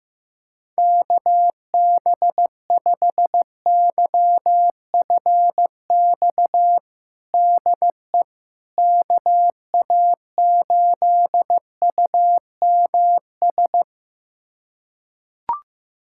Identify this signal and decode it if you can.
Morse code, heard as a keyed tone, about 15 wpm, 700 Hz. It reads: KB5YFX DE KA8UMS
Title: USA Callsign DE Callsign - 6 Characters - 15wpm